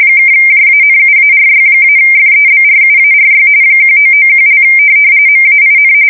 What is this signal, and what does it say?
RTTY, about 45.45 baud. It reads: (JANUARY-FEBRUARY 1991) MAIN
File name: fsk170 / USS-Missouri